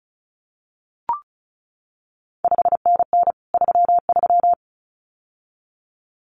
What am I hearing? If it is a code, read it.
Morse code, heard as a keyed tone, about 36 wpm, 700 Hz. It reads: 5NN 33
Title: CQ WW DX Contest Exchange - 35wpm